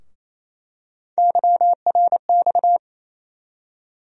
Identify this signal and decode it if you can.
Morse code, heard as a keyed tone, about 28 wpm, 700 Hz. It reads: YRX
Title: USA Call Sign Postfixes - 3 Letters 28wpm